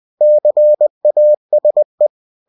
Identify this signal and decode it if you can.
Morse code, heard as a keyed tone, about 20 wpm, 600 Hz. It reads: CASE
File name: cw-books / winnie-the-pooh / winnie-20wpm-page-0111